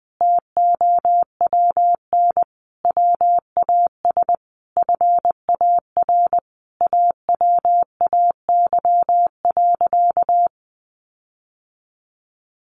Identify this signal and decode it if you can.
Morse code, heard as a keyed tone, about 20 wpm, 700 Hz. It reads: TOWN WAS FAR AWAY.
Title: Sentences from Top 1500 Words 20wpm